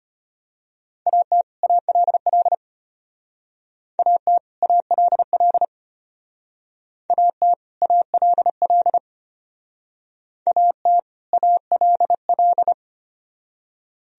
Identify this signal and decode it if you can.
Morse code, heard as a keyed tone, about 32 wpm, 700 Hz. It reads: AT ALL AT ALL AT ALL AT ALL